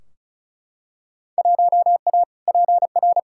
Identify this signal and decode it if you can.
Morse code, heard as a keyed tone, about 34 wpm, 700 Hz. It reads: 1A PR
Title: Field Day Exchanges Encore - 35wpm